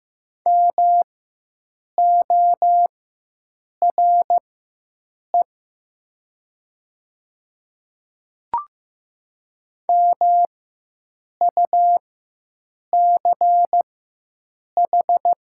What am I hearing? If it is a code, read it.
Morse code, heard as a keyed tone, about 15 wpm, spaced as for 7 wpm, 700 Hz. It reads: MORE MUCH
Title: 4x Character Spacing - Top 300 Words - 15wpm